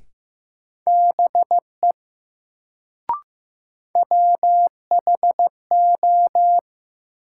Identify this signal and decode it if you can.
Morse code, heard as a keyed tone, about 15 wpm, 700 Hz. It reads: BE WHO
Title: Sentences from Top 100 Words - New Word Drill - 15wpm